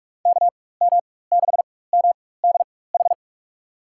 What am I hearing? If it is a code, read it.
Morse code, heard as a keyed tone, about 47 wpm, spaced as for 21 wpm, 700 Hz. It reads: KM6MDH
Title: USA Call Signs - 6 Characters - 4x Character Spacing - 45wpm